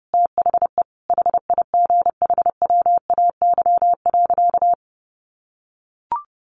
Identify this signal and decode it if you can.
Morse code, heard as a keyed tone, about 30 wpm, 700 Hz. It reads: THE HIGHWAY.
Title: Sentences from Top 1400 Words 30wpm